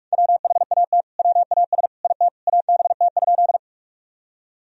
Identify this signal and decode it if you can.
Morse code, heard as a keyed tone, about 46 wpm, 700 Hz. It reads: WHAT WAS IT ABT?
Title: Sentences from Top 100 Words and Abbreviations 45wpm